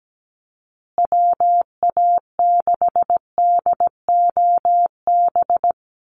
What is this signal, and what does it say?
Morse code, heard as a keyed tone, about 17 wpm, 700 Hz. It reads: WA6DOB
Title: USA Call Signs Repeated 2x 17wpm